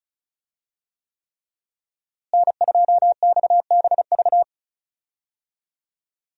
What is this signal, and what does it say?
Morse code, heard as a keyed tone, about 34 wpm, 700 Hz. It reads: N2XBV
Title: USA Call Signs - 5 Characters - Repeated 3x - 35wpm